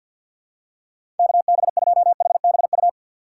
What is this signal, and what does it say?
Morse code, heard as a keyed tone, about 53 wpm, 700 Hz. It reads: KB2HBU
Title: USA Call Signs - 6x Character Spacing - 50wpm